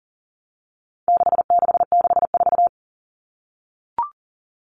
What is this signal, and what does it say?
Morse code, heard as a keyed tone, about 40 wpm, 700 Hz. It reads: 6664
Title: Numbers - 4-Digit 40wpm